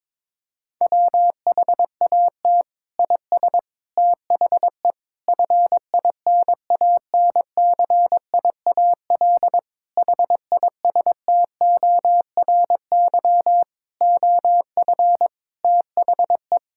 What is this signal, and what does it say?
Morse code, heard as a keyed tone, about 22 wpm, 700 Hz. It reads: WHAT IS THE FINANCIAL HISTORY OF THE
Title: Sentences from Top 1000 Words - Review Part 5 - 22wpm